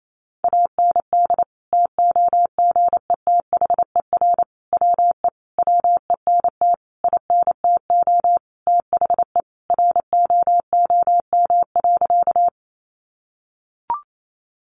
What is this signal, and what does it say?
Morse code, heard as a keyed tone, about 28 wpm, 700 Hz. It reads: AND TOGETHER WE WENT INTO THE ROOM.
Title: Sentences from Top 600 Words 28wpm